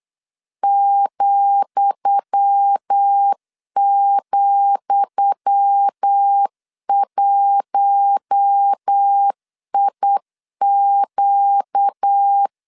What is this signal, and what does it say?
Morse code, heard as a keyed tone, about 8 wpm, 800 Hz. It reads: ,,1IQ
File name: LEZIONI / 10 SIGNS / 10D COMA 8wpm 3749